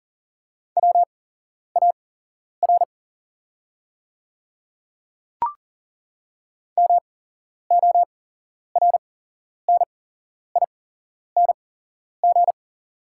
Morse code, heard as a keyed tone, about 40 wpm, spaced as for 11 wpm, 690 Hz. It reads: WAR MORNING
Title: 8x Character Spacing - Top 500 Words - 40wpm